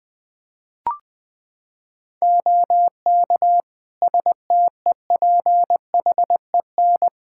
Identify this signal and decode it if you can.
Morse code, heard as a keyed tone, about 21 wpm, 700 Hz. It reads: OK STEPHEN
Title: K1USN SST QSO Practice 20wpm